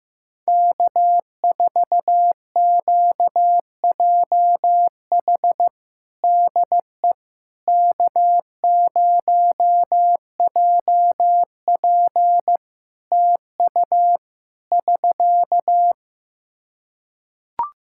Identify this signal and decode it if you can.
Morse code, heard as a keyed tone, about 15 wpm, 700 Hz. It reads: K4QJH DE K0JP TU <SK>
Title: Practice QSO - Exchange 4 - Ending - 15wpm